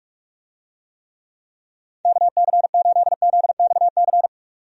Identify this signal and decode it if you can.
Morse code, heard as a keyed tone, about 44 wpm, 700 Hz. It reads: KC8ZXC
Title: USA Call Signs - 6 Characters - 45wpm